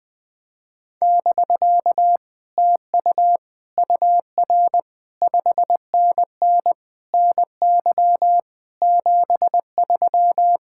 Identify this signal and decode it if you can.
Morse code, heard as a keyed tone, about 20 wpm, 700 Hz. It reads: <BK> TU UR 5NN NY 73
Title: Speed-Racing Common QSO Phrases - 17wpm